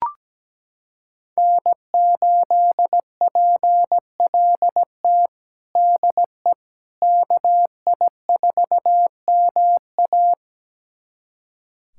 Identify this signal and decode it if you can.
Morse code, heard as a keyed tone, about 17 wpm, 700 Hz. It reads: N8PLT DE KI4MA
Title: USA Callsign DE Callsign - 5 Characters - 17wpm